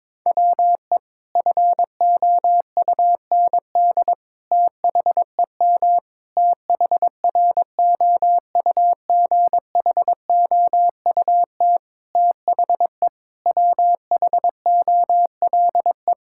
Morse code, heard as a keyed tone, about 22 wpm, 700 Hz. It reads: WE FOUND THEM THROUGHOUT THE WHOLE
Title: Sentences from Top 1000 Words - Review Part 5 - 22wpm